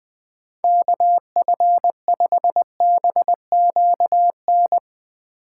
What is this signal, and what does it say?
Morse code, heard as a keyed tone, about 22 wpm, 700 Hz. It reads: KF5BQN
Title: Speed Racing - USA 2x3 Call Signs 20wpm